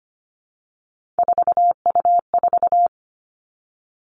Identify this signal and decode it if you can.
Morse code, heard as a keyed tone, about 25 wpm, 700 Hz. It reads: 4U4
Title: Three Challenging Letter-Number Combinations 25wpm